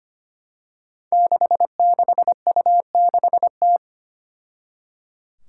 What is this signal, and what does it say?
Morse code, heard as a keyed tone, about 25 wpm, 700 Hz. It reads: 66U6T
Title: Five Challenging Letter-Number Combinations 25wpm